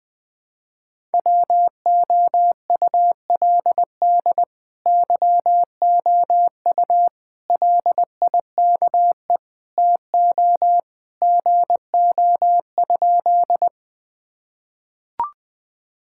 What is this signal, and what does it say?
Morse code, heard as a keyed tone, about 20 wpm, 700 Hz. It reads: WOULD YOU LIKE TO GO?
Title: Sentences with 2-Word Phrases - 20wpm